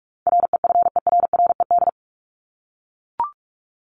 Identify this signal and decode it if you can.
Morse code, heard as a keyed tone, about 44 wpm, 700 Hz. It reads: REFERRED
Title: ED Suffix - Top 100 Words with Base Words 45wpm